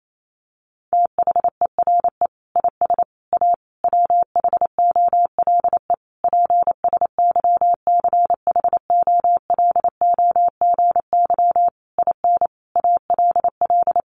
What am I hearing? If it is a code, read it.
Morse code, heard as a keyed tone, about 28 wpm, 700 Hz. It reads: THERE IS A WHOLE PSYCHOLOGY IN ALL
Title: Sentences from Top 1400 Words 28wpm